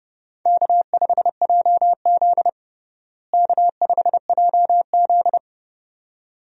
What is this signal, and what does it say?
Morse code, heard as a keyed tone, about 32 wpm, 700 Hz. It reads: K5JZ K5JZ
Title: USA Call Signs - 4 Characters - Repeated 2x - 30wpm